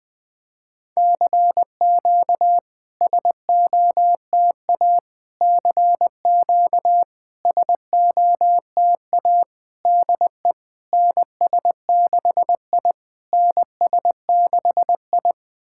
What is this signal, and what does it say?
Morse code, heard as a keyed tone, about 20 wpm, 690 Hz. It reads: CQ SOTA CQ SOTA DE NS6I NS6I
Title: SOTA QSO Practice 20wpm